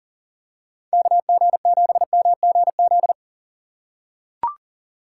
Morse code, heard as a keyed tone, about 40 wpm, 700 Hz. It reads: KG7MGZ
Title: USA Call Signs - 2x Character Spacing - 40wpm